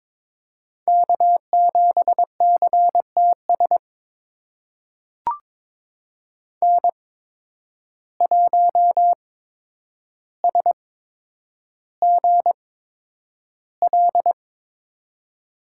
Morse code, heard as a keyed tone, about 22 wpm, spaced as for 9 wpm, 700 Hz. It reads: K7CTS N1SGL
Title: USA Call Signs - 5 Characters - 8x Character Spacing - 22wpm